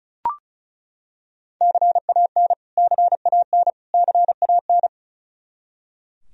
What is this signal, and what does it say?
Morse code, heard as a keyed tone, about 36 wpm, 700 Hz. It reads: CAN CAN CAN
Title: Top 100 Words Repeated 35wpm